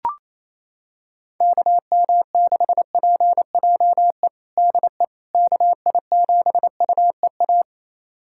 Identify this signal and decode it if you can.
Morse code, heard as a keyed tone, about 28 wpm, 700 Hz. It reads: KM6PJE DE KI7UEA
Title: USA Callsign DE Callsign 28wpm